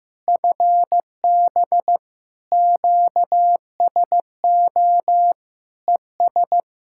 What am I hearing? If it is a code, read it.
Morse code, heard as a keyed tone, about 15 wpm, 700 Hz. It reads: FB QSO ES
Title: Practice QSO - Exchange 4 - Ending - 15wpm